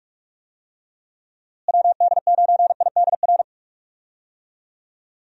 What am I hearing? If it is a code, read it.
Morse code, heard as a keyed tone, about 46 wpm, 700 Hz. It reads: WD9IDR
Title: USA Call Signs - 6 Characters - Repeated 2x - 45wpm